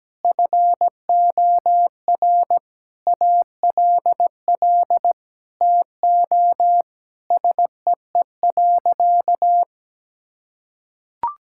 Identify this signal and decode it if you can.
Morse code, heard as a keyed tone, about 17 wpm, 700 Hz. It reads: FOR ALL TO SEE.
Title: Sentences from Top 1000 Words - Review Part 5 - 17wpm